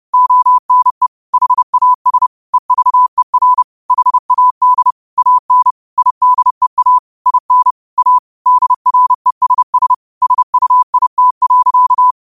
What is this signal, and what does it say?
Morse code, heard as a keyed tone, about 30 wpm, 1000 Hz. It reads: ONE HAS EVER HAD AN IDEA IN A DRESS SUIT.